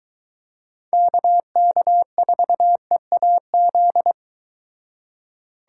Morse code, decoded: KK4EAZ

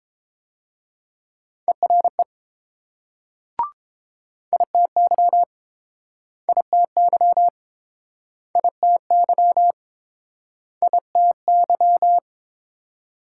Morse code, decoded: ERE ITY ITY ITY ITY

